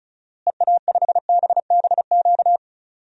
Morse code, heard as a keyed tone, about 36 wpm, 700 Hz. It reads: EA5BBQ
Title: DXCC-100 Call Signs - 6 Characters - Repeated 2x - 35wpm